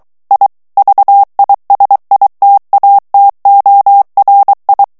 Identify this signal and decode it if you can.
Morse code, heard as a keyed tone, about 23 wpm, 800 Hz. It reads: I VISITATORI
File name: morse mp3 / ITALIEN texte clair 1 vitesse 22 mots minute